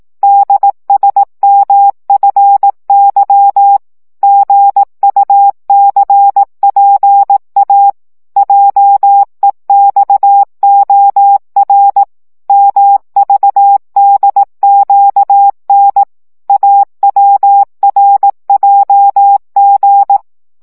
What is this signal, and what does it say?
Morse code, heard as a keyed tone, about 18 wpm, 800 Hz. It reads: DSMFY GUCPA JEXOR MVDQN AWRJG
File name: morze vizsga 2011-1 15